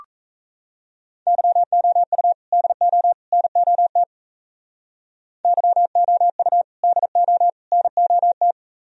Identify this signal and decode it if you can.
Morse code, heard as a keyed tone, about 36 wpm, 700 Hz. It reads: YOU DO NOT YOU DO NOT